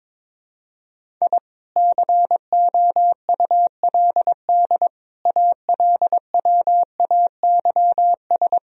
Morse code, heard as a keyed tone, about 22 wpm, 700 Hz. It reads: I COULD ALWAYS
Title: Sentences from Top 1000 Words - Review Part 1 - 22wpm